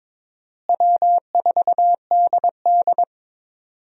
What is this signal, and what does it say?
Morse code, heard as a keyed tone, about 22 wpm, 700 Hz. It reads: W4DD
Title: USA Call Signs - 4 Characters - Repeated 3x - 22wpm